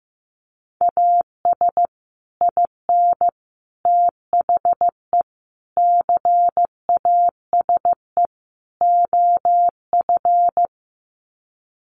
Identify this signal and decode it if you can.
Morse code, heard as a keyed tone, about 15 wpm, 700 Hz. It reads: AS IN THE CASE OF
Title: Sets of 5 Words 15wpm